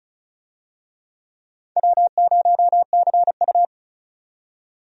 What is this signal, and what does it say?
Morse code, heard as a keyed tone, about 36 wpm, 690 Hz. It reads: W0CU
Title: USA Call Signs - 4 Characters - 35wpm